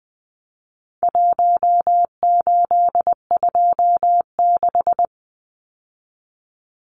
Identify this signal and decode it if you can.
Morse code, heard as a keyed tone, about 20 wpm, 700 Hz. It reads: 1826
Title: Numbers - Contest Style - 4-Digit 20wpm